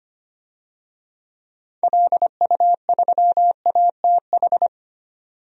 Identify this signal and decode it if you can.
Morse code, heard as a keyed tone, about 25 wpm, 700 Hz. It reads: LU3ATH
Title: DXCC-100 Call Signs - 6 Characters - 25wpm